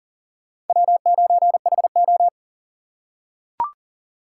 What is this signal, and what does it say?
Morse code, heard as a keyed tone, about 40 wpm, 700 Hz. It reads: W9HO
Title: USA Call Signs - 4 Characters - 3x Character Spacing - 40wpm